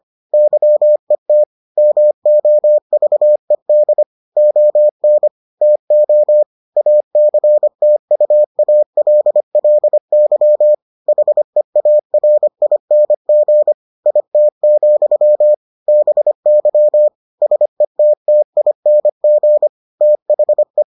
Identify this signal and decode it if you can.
Morse code, heard as a keyed tone, about 25 wpm, 600 Hz. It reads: YET MOVED ON TO ACTUALLY HEARING IT, BY SETTING THE